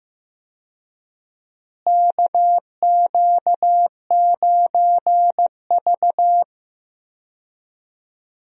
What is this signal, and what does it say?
Morse code, heard as a keyed tone, about 15 wpm, 700 Hz. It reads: KQ9V
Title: USA Call Signs - 4 Characters - Repeated 3x - 15wpm